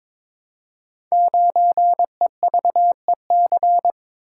Morse code, heard as a keyed tone, about 22 wpm, 700 Hz. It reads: 9EVEC
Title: Code Groups - 5 Letter:Numbers - 22wpm